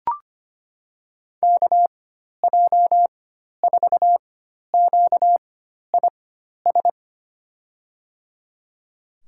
Morse code, heard as a keyed tone, about 25 wpm, spaced as for 12 wpm, 700 Hz. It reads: KJ4QIS
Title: USA Call Signs - 6 Characters - 4x Character Spacing - 25wpm